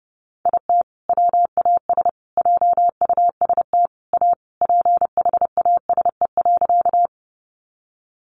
Morse code, heard as a keyed tone, about 30 wpm, 700 Hz. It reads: IT WAS JUST A PHASE.
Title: Sentences from Top 1200 Words 30wpm